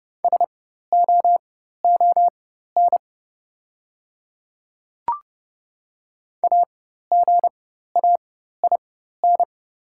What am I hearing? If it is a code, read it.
Morse code, heard as a keyed tone, about 30 wpm, spaced as for 14 wpm, 700 Hz. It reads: SOON AGAIN
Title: 4x Character Spacing - Top 500 Words - 30wpm